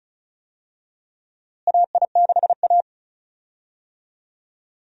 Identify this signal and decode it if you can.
Morse code, heard as a keyed tone, about 36 wpm, 700 Hz. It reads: AI6A
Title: USA Call Signs - 4 Characters - Repeated 2x - 35wpm